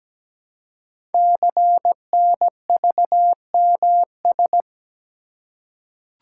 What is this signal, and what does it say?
Morse code, heard as a keyed tone, about 17 wpm, 700 Hz. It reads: CNVMS
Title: Code Groups - 5 Letters - 17wpm